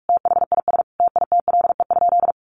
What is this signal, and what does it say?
Morse code, heard as a keyed tone, about 44 wpm, 700 Hz. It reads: THIS TITLE?
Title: Sentences from Top 1200 Words 45wpm